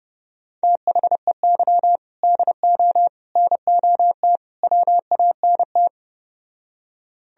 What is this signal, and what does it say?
Morse code, heard as a keyed tone, about 32 wpm, 700 Hz. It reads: THEY DO NOT WANT